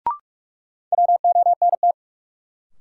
Morse code, heard as a keyed tone, about 46 wpm, 700 Hz. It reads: WONT
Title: Sentences from Top 1000 Words - Review Part 6 - New Word Drill - 45wpm